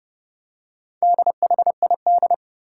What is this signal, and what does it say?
Morse code, heard as a keyed tone, about 32 wpm, 700 Hz. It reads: DHID
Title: Four Challenging Letter-Number Combinations 30wpm